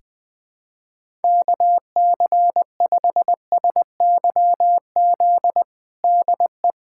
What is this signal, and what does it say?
Morse code, heard as a keyed tone, about 20 wpm, 700 Hz. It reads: KC5SYZ DE
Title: USA Callsign DE Callsign - 6 Characters - 20wpm